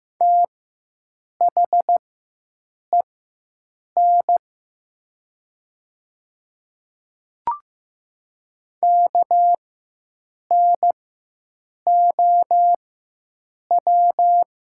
Morse code, heard as a keyed tone, about 15 wpm, spaced as for 7 wpm, 700 Hz. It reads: THEN KNOW
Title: 4x Character Spacing - Top 100 Words - 15wpm